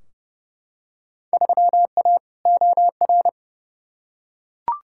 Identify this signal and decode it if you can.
Morse code, heard as a keyed tone, about 30 wpm, 700 Hz. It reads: 3A OR
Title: Field Day Exchanges - 30wpm